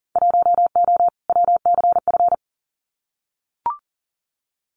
Morse code, heard as a keyed tone, about 40 wpm, 700 Hz. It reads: 1O WCF
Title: Winter Field Day Exchanges 40wpm